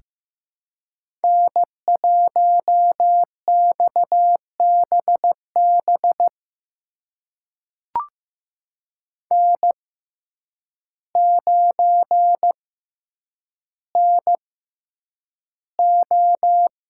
Morse code, heard as a keyed tone, about 15 wpm, spaced as for 8 wpm, 700 Hz. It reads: N1XBB N9NO
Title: USA Call Signs - 5 Characters - 6x Character Spacing - 15wpm